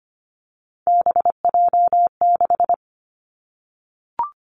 Morse code, heard as a keyed tone, about 25 wpm, 700 Hz. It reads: BJ6
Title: Three Challenging Letter-Number Combinations 25wpm